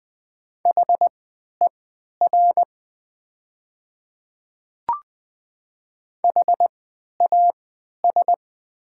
Morse code, heard as a keyed tone, about 20 wpm, spaced as for 11 wpm, 700 Hz. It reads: HER HAS
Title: 3x Character Spacing - Top 100 Words - 20wpm